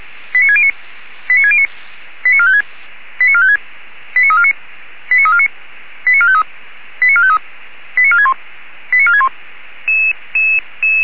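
Five-tone selective call (ZVEI), heard as quick runs of five tones, tones of about 70 ms. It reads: 89609 89609 89456 89456 89349 89349 89453 89453 89461 89461
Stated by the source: ZVEI five-tone selective call 89348 received on an SDR.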